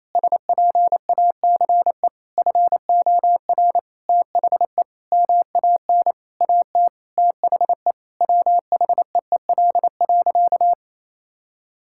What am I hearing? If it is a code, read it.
Morse code, heard as a keyed tone, about 27 wpm, 700 Hz. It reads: SPACE FOR THE MAN AT THE WHEEL.
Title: Sentences from Top 1000 Words - Review Part 5 - 28wpm